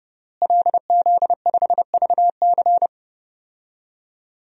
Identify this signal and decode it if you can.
Morse code, heard as a keyed tone, about 30 wpm, 700 Hz. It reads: LZ5VC
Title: DXCC-100 Call Signs - 5 Characters - Repeated 3x - 30wpm